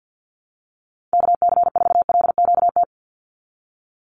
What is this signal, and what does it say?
Morse code, heard as a keyed tone, about 50 wpm, 700 Hz. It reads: KC4LQT
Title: USA Call Signs 50wpm